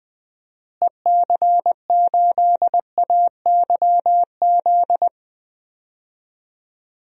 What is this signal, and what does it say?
Morse code, heard as a keyed tone, about 21 wpm, 700 Hz. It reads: EC8AYZ
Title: DXCC-100 Call Signs - 6 Characters - Repeated 3x - 20wpm